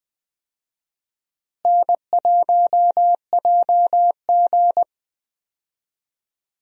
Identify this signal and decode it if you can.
Morse code, heard as a keyed tone, about 20 wpm, 700 Hz. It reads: N1JG
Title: USA Call Signs - 4 Characters - Repeated 3x - 20wpm